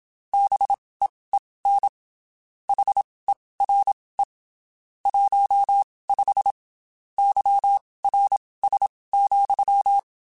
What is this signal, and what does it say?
Morse code, heard as a keyed tone, about 27 wpm, spaced as for 19 wpm, 800 Hz. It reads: BEEN HERE 15 YRS,